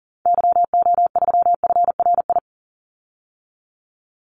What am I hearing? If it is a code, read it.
Morse code, heard as a keyed tone, about 40 wpm, 700 Hz. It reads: YO3FRI
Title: Call Signs Repeated 3x - 40wpm